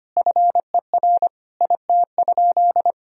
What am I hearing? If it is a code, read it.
Morse code, heard as a keyed tone, about 25 wpm, 700 Hz. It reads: FER IT?
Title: Sentences from Top 400 Words and Abbreviations 25wpm